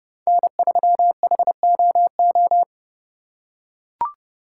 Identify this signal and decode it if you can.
Morse code, heard as a keyed tone, about 31 wpm, 700 Hz. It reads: N3HOO
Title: USA Call Signs - 5 Characters - 2x Character Spacing - 30wpm